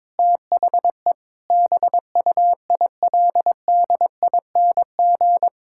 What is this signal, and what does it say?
Morse code, heard as a keyed tone, about 22 wpm, 700 Hz. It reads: THE BUILDING
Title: Sentences from Top 1000 Words - Review Part 5 - 22wpm